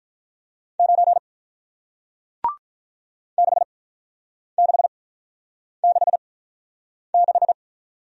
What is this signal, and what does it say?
Morse code, heard as a keyed tone, about 47 wpm, spaced as for 9 wpm, 700 Hz. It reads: 9 6666